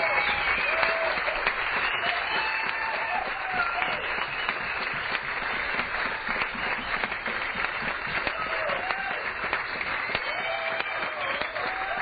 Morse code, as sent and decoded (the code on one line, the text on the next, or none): none